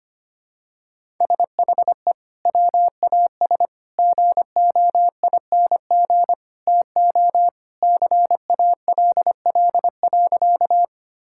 Morse code, heard as a keyed tone, about 25 wpm, 700 Hz. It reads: SHE WAS GOING TO CALL.
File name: practice / 25wpm / Sentences with 3-Word Phrases - 25wpm